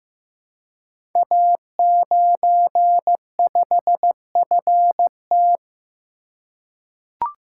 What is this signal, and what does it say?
Morse code, heard as a keyed tone, about 15 wpm, 700 Hz. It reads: A95FT
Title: Code Groups - 5 Letter:Numbers - 15wpm